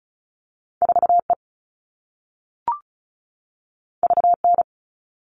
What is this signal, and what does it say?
Morse code, heard as a keyed tone, about 36 wpm, 690 Hz. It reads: 4E VN
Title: All Cut Numbers - 2-Digit 35wpm